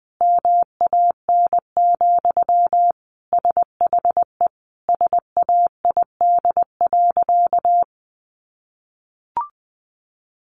Morse code, heard as a keyed tone, about 20 wpm, 700 Hz. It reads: MAN, SHE SAID.
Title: Sentences from Top 1100 Words 20wpm